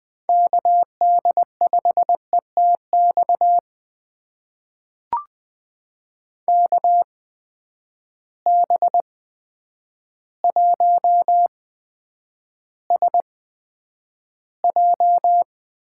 Morse code, heard as a keyed tone, about 20 wpm, spaced as for 9 wpm, 700 Hz. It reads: KD5ETX KB1SJ
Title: USA Call Signs - 6 Characters - 8x Character Spacing - 20wpm